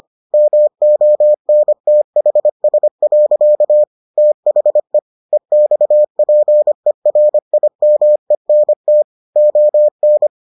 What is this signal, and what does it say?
Morse code, heard as a keyed tone, about 25 wpm, 600 Hz. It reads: MONTHS. THE EXPERIMENT ON